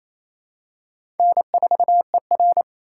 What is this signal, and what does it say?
Morse code, heard as a keyed tone, about 28 wpm, 700 Hz. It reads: N4ER
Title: USA Call Signs - 4 Characters - 4x Character Spacing - 28wpm